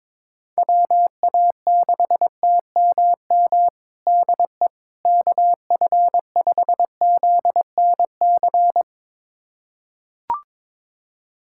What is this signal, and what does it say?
Morse code, heard as a keyed tone, about 22 wpm, 700 Hz. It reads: WA6TMM DE KF5ZNC
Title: USA Callsign DE Callsign - 6 Characters - 22wpm